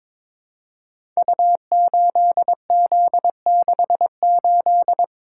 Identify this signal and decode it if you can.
Morse code, heard as a keyed tone, about 22 wpm, 690 Hz. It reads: U8Z68
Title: Code Groups - 5 Letter:Numbers - 22wpm